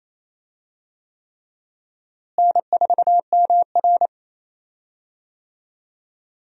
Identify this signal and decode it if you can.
Morse code, heard as a keyed tone, about 27 wpm, 700 Hz. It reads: N4MR
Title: USA Call Signs - 4 Characters - Repeated 3x - 28wpm